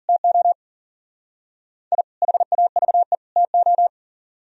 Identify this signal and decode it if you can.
Morse code, heard as a keyed tone, about 52 wpm, 700 Hz. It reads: TO I HAVE TO